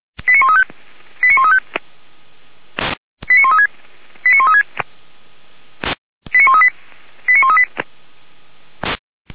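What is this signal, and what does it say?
Five-tone selective call (ZVEI), heard as quick runs of five tones, tones of about 70 ms. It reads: 89135 89135 89136 89136 89138 89138
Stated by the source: ZVEI five-tone call received on SDR.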